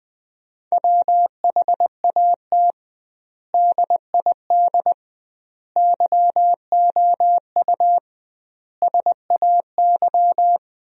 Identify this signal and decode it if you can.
Morse code, heard as a keyed tone, about 20 wpm, 700 Hz. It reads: WHAT DID YOU SAY
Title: 2x Word-Spacing - Sentences from Top 200 Words - 20wpm